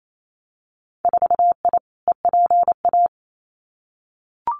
Morse code, heard as a keyed tone, about 28 wpm, 700 Hz. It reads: 4I EPA
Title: Winter Field Day Exchanges Encore 28wpm